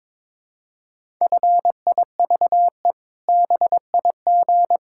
Morse code, heard as a keyed tone, about 22 wpm, 700 Hz. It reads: FIVE BIG
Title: Pangram Challenge v2 - 22wpm